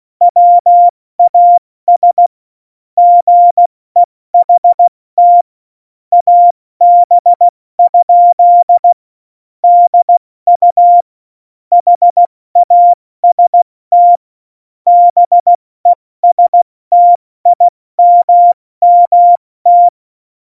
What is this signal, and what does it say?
Morse code, heard as a keyed tone, about 16 wpm, 700 Hz. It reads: WAS GEHT AB? DU HAST BESTIMMT
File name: Morsekurs / Witze / Witz 08. Dezember 2020, 70 F80